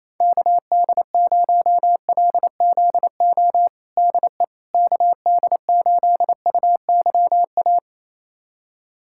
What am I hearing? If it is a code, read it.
Morse code, heard as a keyed tone, about 27 wpm, 700 Hz. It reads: KD0LZO DE KD8UYA
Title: USA Callsign DE Callsign - 6 Characters - 28wpm